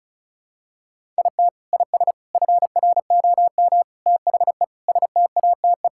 Morse code, heard as a keyed tone, about 36 wpm, 700 Hz. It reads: IT IS FROM THE STATE
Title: Sentences from Top 1600 Words 35wpm